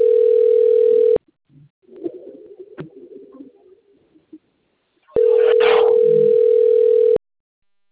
Ringback tone (North America), 440 + 480 Hz.